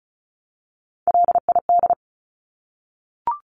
Morse code, heard as a keyed tone, about 34 wpm, 700 Hz. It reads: LID
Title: Abbreviations Repeated 3x - 35wpm